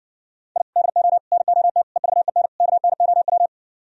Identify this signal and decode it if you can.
Morse code, heard as a keyed tone, about 70 wpm, 700 Hz. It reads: I DO NOT EVEN KNOW